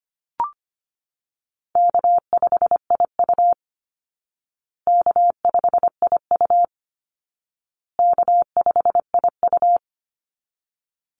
Morse code, heard as a keyed tone, about 25 wpm, 700 Hz. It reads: K5IU K5IU K5IU
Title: Call Signs Repeated 2x - 25wpm